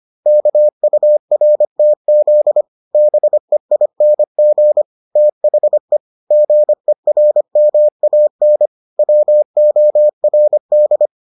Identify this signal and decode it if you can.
Morse code, heard as a keyed tone, about 25 wpm, 600 Hz. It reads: KURTZ BEING THE GERMAN WORD